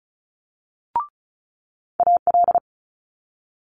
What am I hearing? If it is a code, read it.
Morse code, heard as a keyed tone, about 36 wpm, 700 Hz. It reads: AL
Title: Multi-letter Phonemes Quick Fire 35wpm